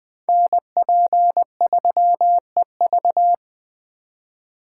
Morse code, heard as a keyed tone, about 20 wpm, 700 Hz. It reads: NP3EV
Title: Speed Racing - USA 2x2 Call Signs 17wpm